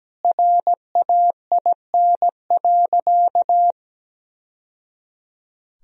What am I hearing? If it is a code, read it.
Morse code, heard as a keyed tone, about 17 wpm, 700 Hz. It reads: RAIN.
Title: Sentences from Top 1000 Words - Review Part 1 - 17wpm